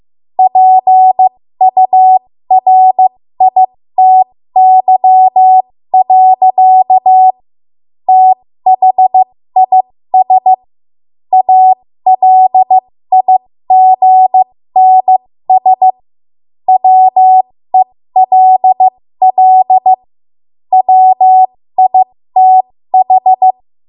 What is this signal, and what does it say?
Morse code, heard as a keyed tone, about 15 wpm, spaced as for 13 wpm, 750 Hz. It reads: PURITY. THIS ALIGNS WELL WITH
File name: Morse / 13WPM